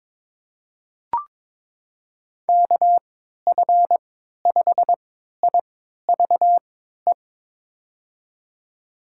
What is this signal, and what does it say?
Morse code, heard as a keyed tone, about 22 wpm, spaced as for 13 wpm, 700 Hz. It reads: KF5IVE
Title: USA Call Signs - 3x Character Spacing - 22wpm